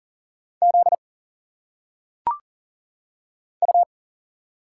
Morse code, heard as a keyed tone, about 38 wpm, 700 Hz. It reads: Z U
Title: Single Letters - Phonetic Pronunciation 40wpm